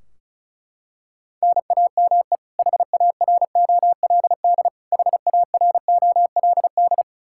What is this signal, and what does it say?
Morse code, heard as a keyed tone, about 36 wpm, 700 Hz. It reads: NAME HAROLD HAROLD
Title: Practice QSO - Exchange 1 - Beginning - 35wpm